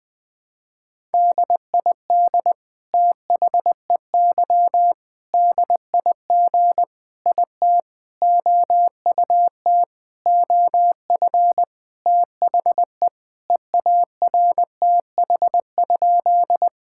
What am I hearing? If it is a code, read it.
Morse code, heard as a keyed tone, about 20 wpm, 700 Hz. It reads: DID THEY DIG IT OUT OF THE EARTH?
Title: Sentences from Top 1600 Words 20wpm